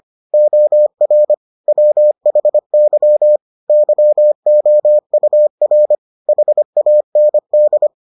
Morse code, heard as a keyed tone, about 25 wpm, 600 Hz. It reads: OR WHY YOUR HAND